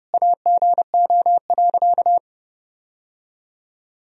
Morse code, heard as a keyed tone, about 32 wpm, 700 Hz. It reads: AGO.
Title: Sentences from Top 1000 Words - Review Part 3 - 30wpm